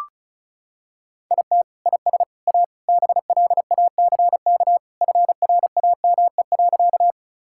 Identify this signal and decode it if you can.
Morse code, heard as a keyed tone, about 36 wpm, 700 Hz. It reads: IT IS A BLACK FRAME.
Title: Sentences from Top 1000 Words - Review Part 6 - 35wpm